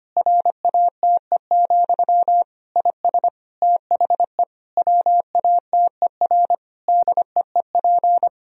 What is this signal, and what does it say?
Morse code, heard as a keyed tone, about 25 wpm, 700 Hz. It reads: RATE, IS THE WATER DEEP
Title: Sentences from Top 1000 Words - Review Part 4 - 25wpm